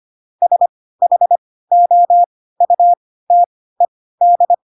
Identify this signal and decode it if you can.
Morse code, heard as a keyed tone, about 25 wpm, spaced as for 16 wpm, 700 Hz. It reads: SHOUTED
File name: Pinocchio / Pinocchio ch2 10 25 ik1hge